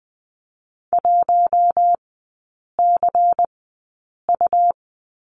Morse code, heard as a keyed tone, about 20 wpm, spaced as for 8 wpm, 700 Hz. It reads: 1CU
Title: 100 Lesson 17e - Letter u - Three Characters 20wpm